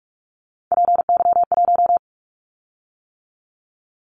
Morse code, heard as a keyed tone, about 46 wpm, 690 Hz. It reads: PY1
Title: Callsign Prefixes for Top 50 DXCC - 45wpm